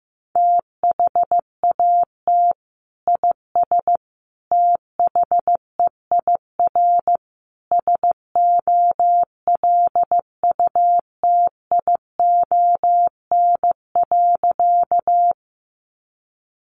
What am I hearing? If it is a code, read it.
Morse code, heard as a keyed tone, about 15 wpm, 700 Hz. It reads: THAT IS THEIR SOLUTION.
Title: Sentences from Top 1300 Words 15wpm